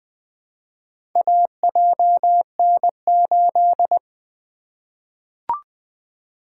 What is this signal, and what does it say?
Morse code, heard as a keyed tone, about 20 wpm, 700 Hz. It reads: AJN8
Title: Four Letter-Number Combinations - 20wpm